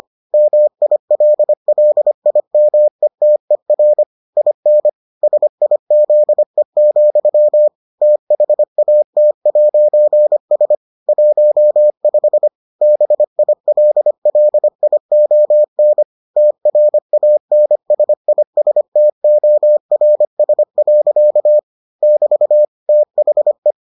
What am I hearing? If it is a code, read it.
Morse code, heard as a keyed tone, about 25 wpm, 600 Hz. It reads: MILLIMETER IN SIZE, THAT'S 15 BILLION TRANSISTORS. <BT> THE